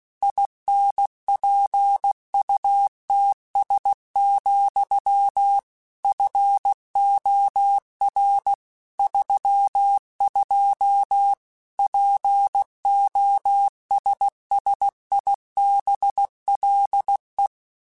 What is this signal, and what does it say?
Morse code, heard as a keyed tone, about 16 wpm, 800 Hz. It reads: INPUTS, FOR 32 POSSIBLE
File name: Technical Subjects / Technical Subjects-mp3s / AND Gate @15